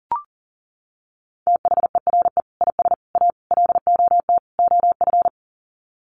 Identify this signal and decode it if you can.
Morse code, heard as a keyed tone, about 40 wpm, 700 Hz. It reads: THERE IS A LOT OF
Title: Sets of 5 Words 40wpm